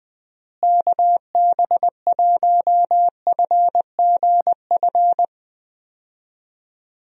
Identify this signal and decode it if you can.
Morse code, heard as a keyed tone, about 20 wpm, 700 Hz. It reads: KB1FGF
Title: USA Call Signs - 6 Characters - Repeated 2x - 20wpm